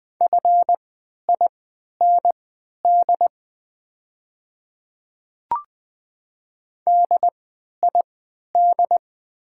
Morse code, heard as a keyed tone, about 20 wpm, spaced as for 11 wpm, 700 Hz. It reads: FIND DID
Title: 3x Character Spacing - Top 100 Words - 20wpm